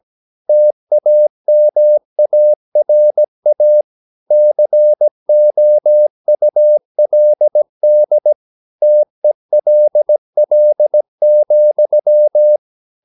Morse code, heard as a keyed tone, about 17 wpm, 600 Hz. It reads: TAMARA COULD TELL,
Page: Romanen Never af Ken Follet på engelsk.